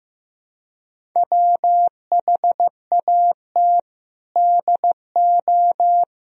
Morse code, heard as a keyed tone, about 16 wpm, 700 Hz. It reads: WHAT DO
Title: Sentences from Top 100 Words and Abbreviations 15wpm